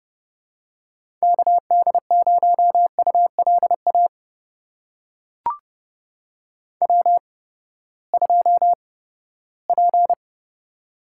Morse code, heard as a keyed tone, about 30 wpm, spaced as for 17 wpm, 700 Hz. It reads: KD0ULA W2P
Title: USA Call Signs - 8x Character Spacing - 30wpm